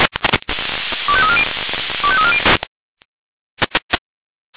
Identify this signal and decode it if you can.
Five-tone selective call (ZVEI), heard as quick runs of five tones, tones of about 70 ms. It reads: 25340 25340